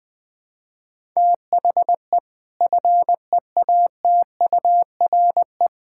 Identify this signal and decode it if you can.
Morse code, heard as a keyed tone, about 20 wpm, 700 Hz. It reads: THE FEATURE
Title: Sentences from Top 1000 Words - Review Part 5 - 20wpm